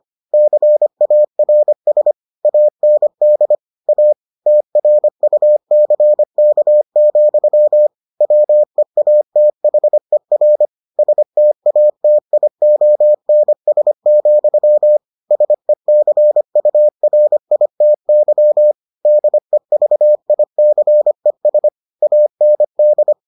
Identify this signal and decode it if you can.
Morse code, heard as a keyed tone, about 25 wpm, 600 Hz. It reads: CARS AND A TRUCK, WEATHER STATIONS, SECURITY DEVICES AND